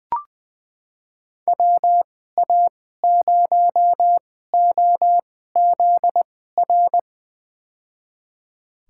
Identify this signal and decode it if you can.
Morse code, heard as a keyed tone, about 20 wpm, spaced as for 14 wpm, 700 Hz. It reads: WA0OZR
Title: USA Call Signs - 6 Characters - 2x Character Spacing - 20wpm